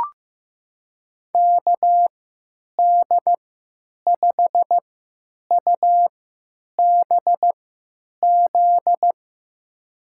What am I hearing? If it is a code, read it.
Morse code, heard as a keyed tone, about 16 wpm, spaced as for 9 wpm, 700 Hz. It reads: KD5UBZ
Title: USA Call Signs - 3x Character Spacing - 15wpm